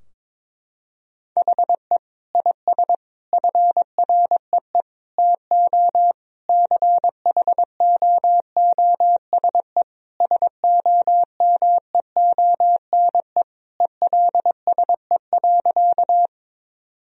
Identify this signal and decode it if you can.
Morse code, heard as a keyed tone, about 22 wpm, 700 Hz. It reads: HE IS FREE TO CHOOSE SOMEONE ELSE.
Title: Sentences from Top 1000 Words - Review Part 4 - 22wpm